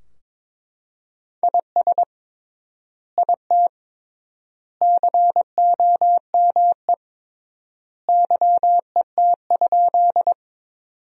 Morse code, heard as a keyed tone, about 22 wpm, 700 Hz. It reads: IS IT COME YET?
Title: 3x Word-Spacing - Sentences from Top 500 Words - 22wpm